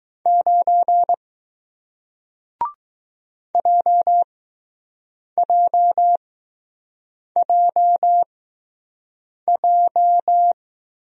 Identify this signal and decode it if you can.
Morse code, heard as a keyed tone, about 20 wpm, spaced as for 6 wpm, 700 Hz. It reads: 9 JJJJ